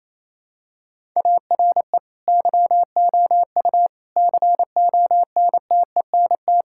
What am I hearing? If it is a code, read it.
Morse code, heard as a keyed tone, about 28 wpm, 700 Hz. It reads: ARE YOU CONTENT
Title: Sentences from Top 1000 Words - Review Part 6 - 28wpm